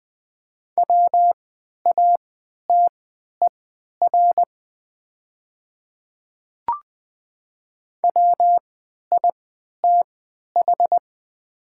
Morse code, heard as a keyed tone, about 20 wpm, spaced as for 11 wpm, 700 Hz. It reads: WATER WITH
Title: 3x Character Spacing - Top 100 Words - 20wpm